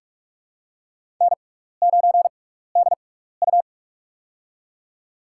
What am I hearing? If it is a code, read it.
Morse code, heard as a keyed tone, about 46 wpm, spaced as for 16 wpm, 700 Hz. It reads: N9DU
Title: USA Call Signs - 4 Characters - 6x Character Spacing - 45wpm